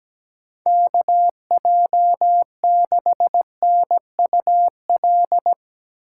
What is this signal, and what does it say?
Morse code, heard as a keyed tone, about 17 wpm, 700 Hz. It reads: KJ6NUL
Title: USA Call Signs - 6 Characters - 2x Character Spacing - 17wpm